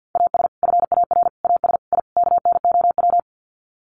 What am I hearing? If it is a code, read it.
Morse code, heard as a keyed tone, about 50 wpm, 700 Hz. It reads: AS FAR AS I KNOW
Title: Sets of 5 Words 50wpm